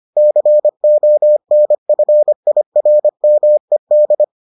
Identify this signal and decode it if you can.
Morse code, heard as a keyed tone, about 25 wpm, 600 Hz. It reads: CONFIRMED